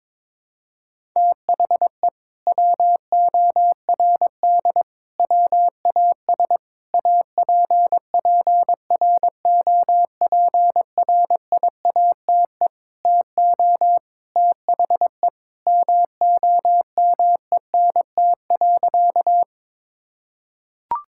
Morse code, heard as a keyed tone, about 22 wpm, 700 Hz. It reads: THE WORD WAS APPROPRIATE TO THE MOMENT.
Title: Sentences from Top 1000 Words - Review Part 5 - 22wpm